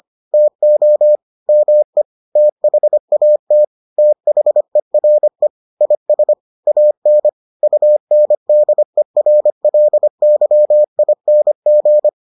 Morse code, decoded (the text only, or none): TO ME THAT THERE IS AN UNDERLYING